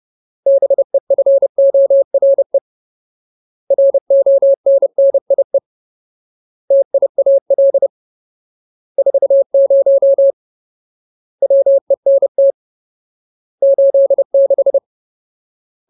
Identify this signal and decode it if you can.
Morse code, heard as a keyed tone, about 30 wpm, 550 Hz. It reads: BEFORE RONNIE TIAL 40 WENT 86
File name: Ultimate Word List / 30WPM Ultimate